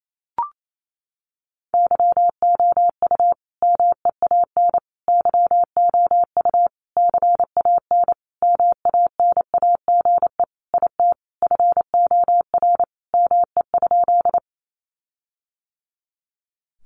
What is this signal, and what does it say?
Morse code, heard as a keyed tone, about 28 wpm, 700 Hz. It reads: YOU MEAN YOU CAN MANAGE IT FOR ME?
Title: Sentences from Top 1400 Words 28wpm